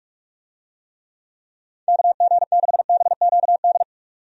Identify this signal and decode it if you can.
Morse code, heard as a keyed tone, about 45 wpm, 700 Hz. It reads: KG6BQD
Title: USA Call Signs - 6 Characters - 45wpm